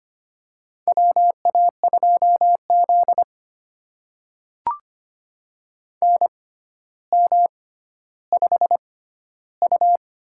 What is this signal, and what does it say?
Morse code, heard as a keyed tone, about 25 wpm, spaced as for 13 wpm, 700 Hz. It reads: WA2Z NM5U
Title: USA Call Signs - 4 Characters - 6x Character Spacing - 25wpm